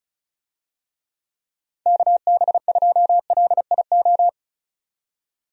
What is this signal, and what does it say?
Morse code, heard as a keyed tone, about 36 wpm, 700 Hz. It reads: KB2LIO
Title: USA Call Signs - 6 Characters - 35wpm